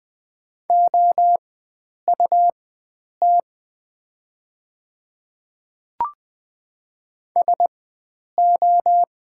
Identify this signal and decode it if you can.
Morse code, heard as a keyed tone, about 20 wpm, spaced as for 9 wpm, 700 Hz. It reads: OUT SO
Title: 4x Character Spacing - Top 100 Words - 20wpm